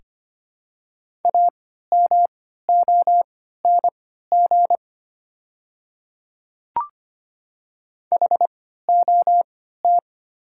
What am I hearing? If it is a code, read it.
Morse code, heard as a keyed tone, about 25 wpm, spaced as for 14 wpm, 700 Hz. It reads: AMONG HOT
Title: 3x Character Spacing - Top 500 Words - 25wpm